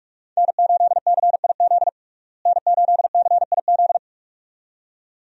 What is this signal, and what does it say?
Morse code, heard as a keyed tone, about 41 wpm, 700 Hz. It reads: N8CIZ N8CIZ
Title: USA Call Signs - 5 Characters - Repeated 2x - 45wpm